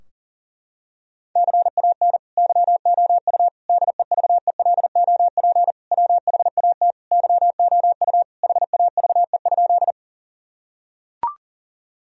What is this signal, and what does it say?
Morse code, heard as a keyed tone, about 40 wpm, 700 Hz. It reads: CAN YOU DEVELOP WHAT YOU HAVE?
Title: Sentences from Top 1000 Words - Review Part 2 - 40wpm